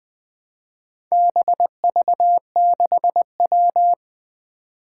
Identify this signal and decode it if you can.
Morse code, heard as a keyed tone, about 20 wpm, 700 Hz. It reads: BV6W
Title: Four Challenging Letter-Number Combinations 20wpm